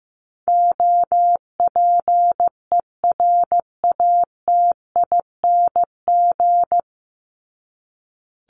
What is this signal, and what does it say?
Morse code, heard as a keyed tone, about 15 wpm, 700 Hz. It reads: OPERATING
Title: ING Suffix - Top 100 Words with Base Words 15wpm